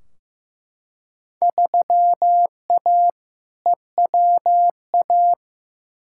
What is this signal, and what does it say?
Morse code, heard as a keyed tone, about 15 wpm, 700 Hz. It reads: 3A EWA
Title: Field Day Exchanges - 15wpm